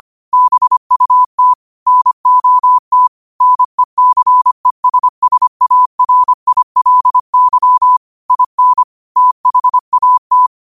Morse code, heard as a keyed tone, about 25 wpm, 1000 Hz. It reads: BUT NOT NECESSARILY IN THAT